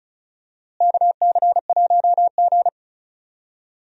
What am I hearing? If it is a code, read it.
Morse code, heard as a keyed tone, about 36 wpm, 700 Hz. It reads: KC1G
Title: USA Call Signs - 4 Characters - Repeated 2x - 35wpm